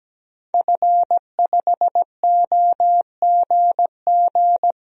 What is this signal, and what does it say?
Morse code, heard as a keyed tone, about 17 wpm, 700 Hz. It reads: F5OGG
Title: DXCC-100 Call Signs - 5 Characters - 17wpm